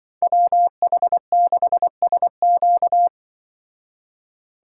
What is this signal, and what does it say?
Morse code, heard as a keyed tone, about 25 wpm, 700 Hz. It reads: WH6SQ
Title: Speed Racing - USA Call Signs 20wpm